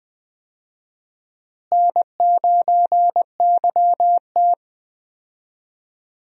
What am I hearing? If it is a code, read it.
Morse code, heard as a keyed tone, about 20 wpm, 700 Hz. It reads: N9YT
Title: USA Call Signs - 4 Characters - Repeated 2x - 20wpm